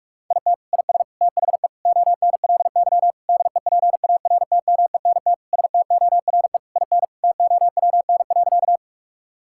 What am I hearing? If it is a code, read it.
Morse code, heard as a keyed tone, about 46 wpm, 700 Hz. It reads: IT IS THE ONLY DEPARTMENT STORE IN TOWN.
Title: Sentences from Top 1000 Words - Review Part 5 - 45wpm